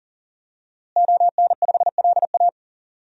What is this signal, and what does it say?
Morse code, heard as a keyed tone, about 40 wpm, 700 Hz. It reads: ON5LA
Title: DXCC-100 Call Signs - 5 Characters - Repeated 3x - 40wpm